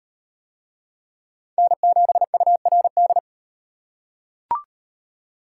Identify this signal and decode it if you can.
Morse code, heard as a keyed tone, about 38 wpm, 700 Hz. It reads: N7URD